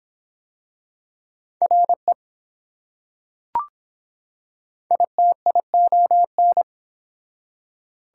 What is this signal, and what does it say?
Morse code, heard as a keyed tone, about 22 wpm, 700 Hz. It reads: RE ITION